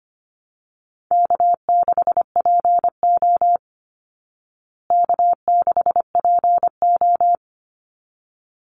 Morse code, heard as a keyed tone, about 25 wpm, 700 Hz. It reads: K6PO K6PO
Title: Easy Call Signs Repeated 2x - 25wpm